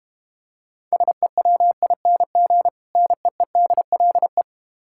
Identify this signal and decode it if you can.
Morse code, heard as a keyed tone, about 30 wpm, 700 Hz. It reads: SEWING NEEDLE
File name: Easy Words - Speed-Racing with Hints - 25wpm